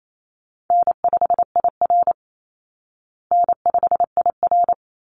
Morse code, heard as a keyed tone, about 28 wpm, 700 Hz. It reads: N5IR N5IR
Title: Call Signs Repeated 3x - 28wpm